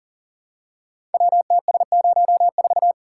Morse code, heard as a keyed tone, about 39 wpm, 700 Hz. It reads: WTS04
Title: Code Groups - 5 Letter:Numbers - 40wpm